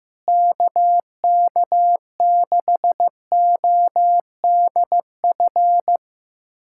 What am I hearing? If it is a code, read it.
Morse code, heard as a keyed tone, about 15 wpm, 700 Hz. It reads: KK6ODF
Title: USA Callsign DE Callsign 15wpm